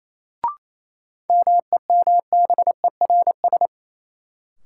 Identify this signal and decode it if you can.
Morse code, heard as a keyed tone, about 28 wpm, 700 Hz. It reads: MEMBERS
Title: Sets of 3 Words - New Word Drill - 28wpm